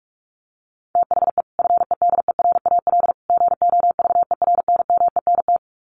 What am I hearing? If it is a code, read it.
Morse code, heard as a keyed tone, about 44 wpm, 700 Hz. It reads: THE FEDERAL GOVERNMENT
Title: Sets of 3 Words 45wpm